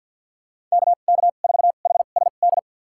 Morse code, heard as a keyed tone, about 52 wpm, spaced as for 37 wpm, 700 Hz. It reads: KK4HSD
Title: USA Call Signs - 2x Character Spacing - 50wpm